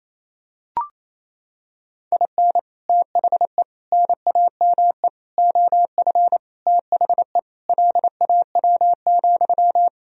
Morse code, heard as a keyed tone, about 28 wpm, 700 Hz. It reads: IN THE NAME OF THE LAW,
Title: Sentences from Top 1000 Words - Review Part 3 - 28wpm